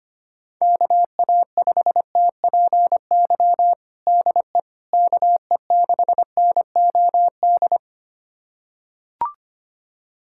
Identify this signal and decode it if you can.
Morse code, heard as a keyed tone, about 25 wpm, 700 Hz. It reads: KA5TPY DE KE6NOD
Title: USA Callsign DE Callsign - 6 Characters - 25wpm